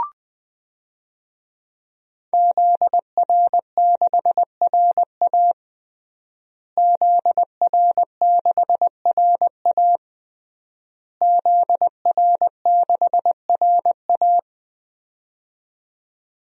Morse code, heard as a keyed tone, about 20 wpm, 700 Hz. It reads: ZR6RA ZR6RA ZR6RA
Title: DXCC-100 Call Signs - 5 Characters - Repeated 3x - 20wpm